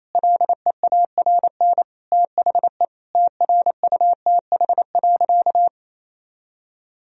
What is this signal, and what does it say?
Morse code, heard as a keyed tone, about 28 wpm, 700 Hz. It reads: LEARN THE TRUTH.
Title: Sentences from Top 1000 Words - Review Part 4 - 28wpm